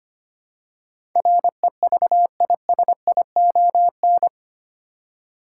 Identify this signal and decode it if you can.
Morse code, heard as a keyed tone, about 25 wpm, 700 Hz. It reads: REVISION
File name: Eight Letter Words - 25wpm